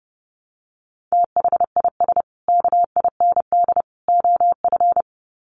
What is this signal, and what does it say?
Morse code, heard as a keyed tone, about 30 wpm, 700 Hz. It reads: THIS KIND OF
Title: Sets of 3 Words 30wpm